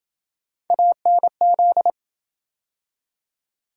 Morse code, heard as a keyed tone, about 28 wpm, 700 Hz. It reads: ANZ